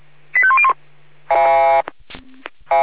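Five-tone selective call (ZVEI), heard as quick runs of five tones, tones of about 70 ms. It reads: 85101
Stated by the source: Five-tone ZVEI selective call 85201 received on an SDR.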